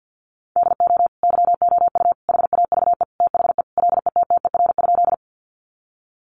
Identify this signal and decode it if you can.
Morse code, heard as a keyed tone, about 49 wpm, 700 Hz. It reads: DO YOU HAVE THE LETTER?
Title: Sentences from Top 300 Words 50wpm